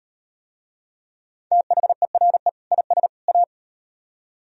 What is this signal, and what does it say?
Morse code, heard as a keyed tone, about 38 wpm, 700 Hz. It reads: THERE IS A